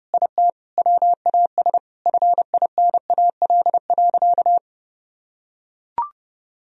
Morse code, heard as a keyed tone, about 30 wpm, 700 Hz. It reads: IT WAS FINAL.
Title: Sentences from Top 1000 Words - Review Part 2 - 30wpm